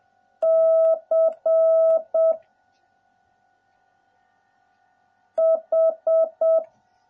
Morse code, heard as a keyed tone, about 7 wpm, 650 Hz. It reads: C H